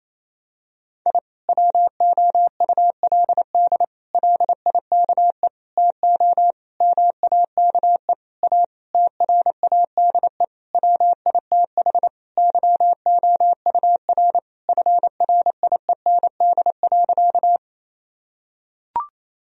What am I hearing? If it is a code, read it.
Morse code, heard as a keyed tone, about 28 wpm, 700 Hz. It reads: I WOULD LIKE TO MAKE A TRADE WITH YOUR FRIEND.
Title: Sentences from Top 1000 Words - Review Part 4 - 28wpm